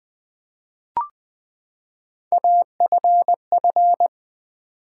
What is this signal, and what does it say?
Morse code, heard as a keyed tone, about 20 wpm, 700 Hz. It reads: AFF
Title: Build A Word - 3 Letters At A Time - 9-Letter Words - 20wpm